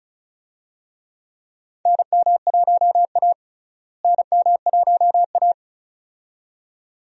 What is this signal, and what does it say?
Morse code, heard as a keyed tone, about 36 wpm, 700 Hz. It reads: NM1A NM1A
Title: USA Call Signs - 4 Characters - Repeated 2x - 35wpm